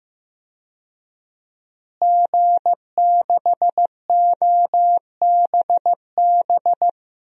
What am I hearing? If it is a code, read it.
Morse code, heard as a keyed tone, about 15 wpm, 700 Hz. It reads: G6OBB
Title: DXCC-100 Call Signs - 5 Characters - Repeated 2x - 15wpm